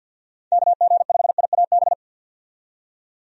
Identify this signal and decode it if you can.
Morse code, heard as a keyed tone, about 51 wpm, 700 Hz. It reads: KG5IAB
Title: USA Call Signs - 6 Characters - 50wpm